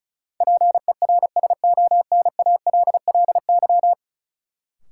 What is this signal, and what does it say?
Morse code, heard as a keyed tone, about 36 wpm, 700 Hz. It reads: PERSONALLY
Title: Sentences from Top 1400 Words - New Word Drill Alt - 35wpm